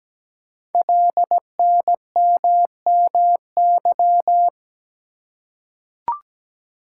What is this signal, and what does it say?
Morse code, heard as a keyed tone, about 17 wpm, 700 Hz. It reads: LNMMY
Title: Code Groups - 5 Letters - 17wpm